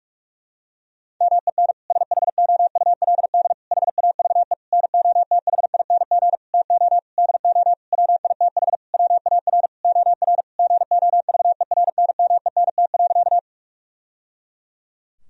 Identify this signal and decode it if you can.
Morse code, heard as a keyed tone, about 41 wpm, 700 Hz. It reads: MEN SHOULD HAVE NOTHING TO DO WITH WAR OR GOVERNMENT.
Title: Sentences from Top 1000 Words - Review Part 3 - 45wpm